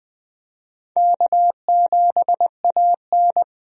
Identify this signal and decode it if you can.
Morse code, heard as a keyed tone, about 20 wpm, 700 Hz. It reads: K7AN
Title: USA Call Signs - 4 Characters - 20wpm